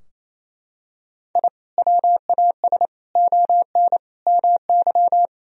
Morse code, heard as a keyed tone, about 28 wpm, 700 Hz. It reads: I WAS ON MY
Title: Sentences from Top 1000 Words - Review Part 3 - 28wpm